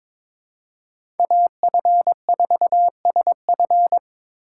Morse code, heard as a keyed tone, about 22 wpm, 700 Hz. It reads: AF4SF
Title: USA Call Signs - 5 Characters - 4x Character Spacing - 22wpm